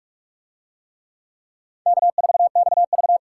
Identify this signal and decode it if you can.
Morse code, heard as a keyed tone, about 42 wpm, 700 Hz. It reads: K4XV